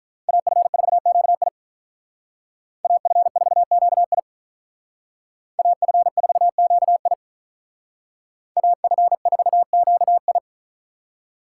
Morse code, heard as a keyed tone, about 46 wpm, 700 Hz. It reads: AF4QI AF4QI AF4QI AF4QI